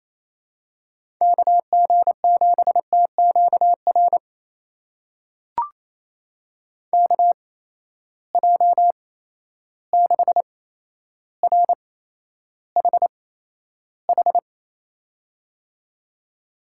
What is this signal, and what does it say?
Morse code, heard as a keyed tone, about 28 wpm, spaced as for 12 wpm, 700 Hz. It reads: KG7TQR KJ6RHH
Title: USA Call Signs - 6 Characters - 8x Character Spacing - 28wpm